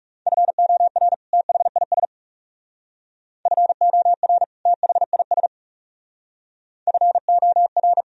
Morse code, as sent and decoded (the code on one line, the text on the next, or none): ..-. --- .-. / - .... .. ... / ..-. --- .-. / - .... .. ... / ..-. --- .-.
FOR THIS FOR THIS FOR